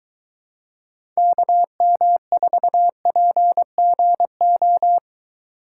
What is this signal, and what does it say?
Morse code, heard as a keyed tone, about 23 wpm, 700 Hz. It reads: KM4PGO